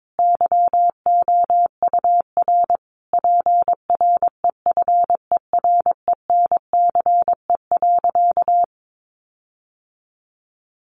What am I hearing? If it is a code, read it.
Morse code, heard as a keyed tone, about 22 wpm, 700 Hz. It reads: YOUR PREFERENCE.
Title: Sentences from Top 1500 Words 22wpm